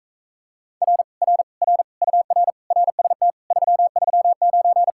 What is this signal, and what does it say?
Morse code, heard as a keyed tone, about 37 wpm, 700 Hz. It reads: R R R UR RST 339